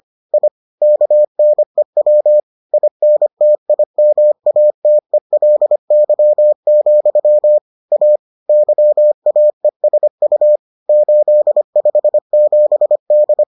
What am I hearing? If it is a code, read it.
Morse code, heard as a keyed tone, about 25 wpm, 600 Hz. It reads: I KNEW INTIMATELY, A YAESU 857D